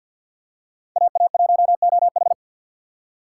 Morse code, heard as a keyed tone, about 52 wpm, 700 Hz. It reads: AA1OH
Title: USA Call Signs - 5 Characters - 4x Character Spacing - 50wpm